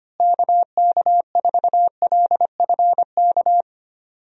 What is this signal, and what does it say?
Morse code, heard as a keyed tone, about 25 wpm, 700 Hz. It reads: KK4LFK
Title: USA Call Signs - 6 Characters - Repeated 3x - 25wpm